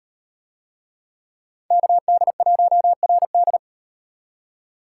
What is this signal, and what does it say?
Morse code, heard as a keyed tone, about 38 wpm, 700 Hz. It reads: KD1RD